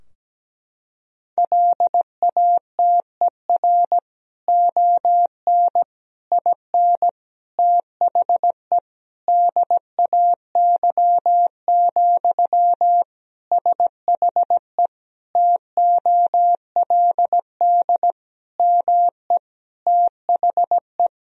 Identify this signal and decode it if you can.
Morse code, heard as a keyed tone, about 17 wpm, 700 Hz. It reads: LATER ON IN THE DAY, SHE TOLD ME THE
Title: Sentences from Top 1000 Words - Review Part 4 - 17wpm